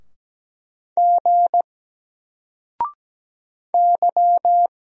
Morse code, heard as a keyed tone, about 17 wpm, 700 Hz. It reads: G Y
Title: Single Character (Letter:Number:Punctuation:Prosign) Alt 17wpm